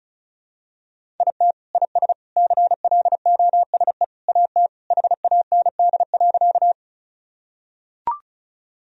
Morse code, heard as a keyed tone, about 34 wpm, 700 Hz. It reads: IT IS CLOSE AT HAND.
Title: Sentences from Top 1000 Words - Review Part 1 - 35wpm